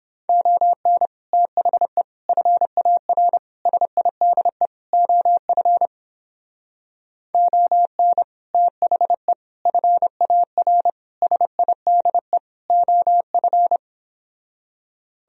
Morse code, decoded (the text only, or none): ON THE FAR SIDE OF ON THE FAR SIDE OF